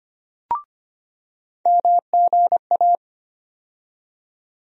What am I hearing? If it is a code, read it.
Morse code, heard as a keyed tone, about 25 wpm, 700 Hz. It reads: MGA
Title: USA Call Sign Postfixes - 3 Letters 25wpm